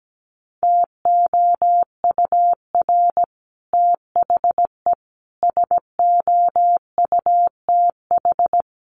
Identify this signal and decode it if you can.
Morse code, heard as a keyed tone, about 17 wpm, 700 Hz. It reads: TOUR THE SOUTH
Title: Sentences from Top 1400 Words 17wpm